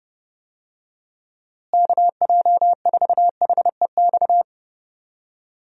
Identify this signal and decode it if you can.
Morse code, heard as a keyed tone, about 32 wpm, 700 Hz. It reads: KJ4HEX
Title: USA Call Signs - 6 Characters - Repeated 2x - 30wpm